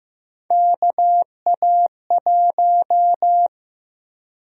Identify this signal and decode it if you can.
Morse code, heard as a keyed tone, about 15 wpm, 700 Hz. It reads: KA1
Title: USA Call Sign Prefixes - 15wpm